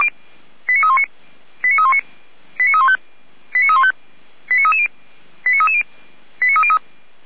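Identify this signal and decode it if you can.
Five-tone selective call (ZVEI), heard as quick runs of five tones, tones of about 70 ms. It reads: 89319 89319 89315 89315 89330 89330 89303
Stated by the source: ZVEI five-tone selective call 89125 received with an SDR.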